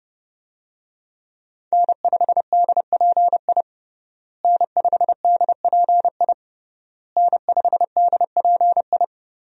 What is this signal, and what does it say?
Morse code, heard as a keyed tone, about 30 wpm, 700 Hz. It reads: N5DPI N5DPI N5DPI
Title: USA Call Signs - 5 Characters - Repeated 3x - 30wpm